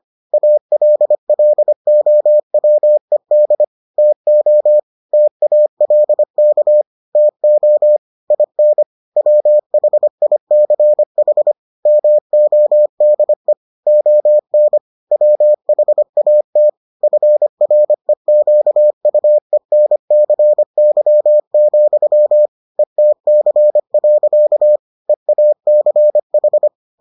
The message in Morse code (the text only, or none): ALLOWED TO TALK TO IN WHICH MODE ON WHAT FREQUENCY, ETC. EACH